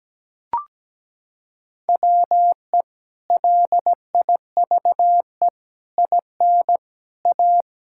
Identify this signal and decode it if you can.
Morse code, heard as a keyed tone, about 17 wpm, 700 Hz. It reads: WE LIVE IN A
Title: Sentences from Top 1000 Words - Review Part 5 - 17wpm